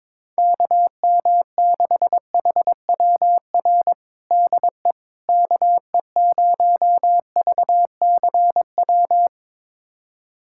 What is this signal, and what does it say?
Morse code, heard as a keyed tone, about 22 wpm, 700 Hz. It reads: KM6HWR DE KE0VCW
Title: USA Callsign DE Callsign - 6 Characters - 22wpm